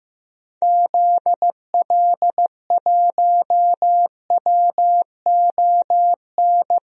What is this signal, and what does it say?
Morse code, heard as a keyed tone, about 15 wpm, 700 Hz. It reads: ZL1WON
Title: DXCC-100 Call Signs - 6 Characters - 15wpm